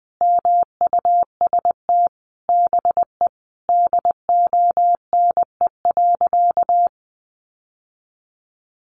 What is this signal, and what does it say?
Morse code, heard as a keyed tone, about 20 wpm, 700 Hz. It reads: MUST BE DONE.
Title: Sentences from Top 1200 Words 20wpm